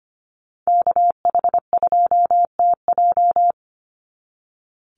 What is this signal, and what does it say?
Morse code, heard as a keyed tone, about 25 wpm, 700 Hz. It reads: KH2TJ
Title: NA SOTA Activator Callsigns 25wpm